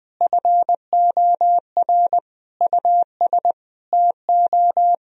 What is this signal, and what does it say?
Morse code, heard as a keyed tone, about 20 wpm, 700 Hz. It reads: FOR US TO
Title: Sentences from Top 1000 Words - Review Part 4 - 20wpm